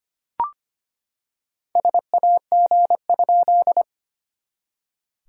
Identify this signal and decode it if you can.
Morse code, heard as a keyed tone, about 25 wpm, 690 Hz. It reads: SAG?
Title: SOTA QSO Practice 25wpm